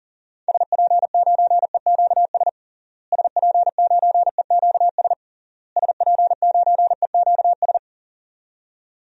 Morse code, heard as a keyed tone, about 40 wpm, 700 Hz. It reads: SP9EQS SP9EQS SP9EQS
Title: DXCC-100 Call Signs - 6 Characters - Repeated 3x - 40wpm